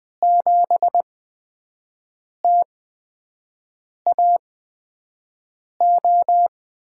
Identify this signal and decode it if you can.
Morse code, heard as a keyed tone, about 20 wpm, spaced as for 5 wpm, 700 Hz. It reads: 7TAO
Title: USA Call Signs - 5 Characters - 8x Character Spacing - 20wpm